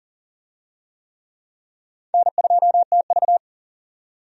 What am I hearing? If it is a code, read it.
Morse code, heard as a keyed tone, about 40 wpm, 700 Hz. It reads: N2TV
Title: USA Call Signs - 4 Characters - 40wpm